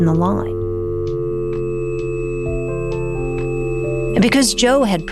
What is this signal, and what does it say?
Dial tone (North America), 350 + 440 Hz, with speech.